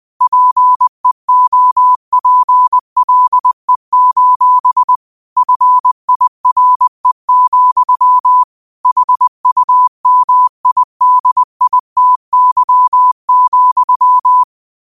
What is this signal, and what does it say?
Morse code, heard as a keyed tone, about 20 wpm, 1000 Hz. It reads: PEOPLE: FIRE, HUMIDITY,
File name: qotdfiles / 20wpm